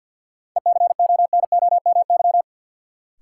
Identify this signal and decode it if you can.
Morse code, heard as a keyed tone, about 52 wpm, 700 Hz. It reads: ECONOMY
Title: Sentences from Top 1200 Words - New Word Drill Alt - 50wpm